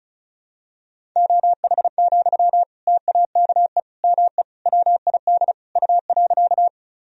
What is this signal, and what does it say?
Morse code, heard as a keyed tone, about 34 wpm, 700 Hz. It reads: OH, TAKE ME WID U.
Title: Sentences from Top 500 Words and Abbreviations 35wpm